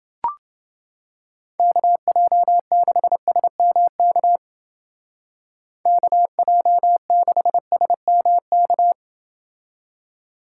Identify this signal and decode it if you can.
Morse code, heard as a keyed tone, about 29 wpm, 700 Hz. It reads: KJ6SMK KJ6SMK